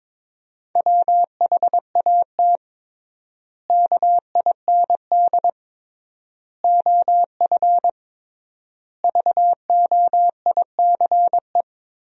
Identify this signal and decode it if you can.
Morse code, heard as a keyed tone, about 21 wpm, 700 Hz. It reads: WHAT KIND OF VOICE
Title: 3x Word-Spacing - Sentences from Top 500 Words - 22wpm